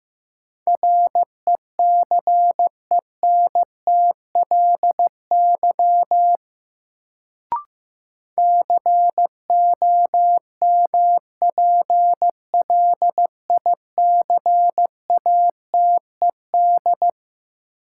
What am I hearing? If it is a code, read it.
Morse code, heard as a keyed tone, about 15 wpm, 700 Hz. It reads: RECENTLY COMPLICATED
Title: Sentences from Top 1200 Words - New Word Drill Alt - 15wpm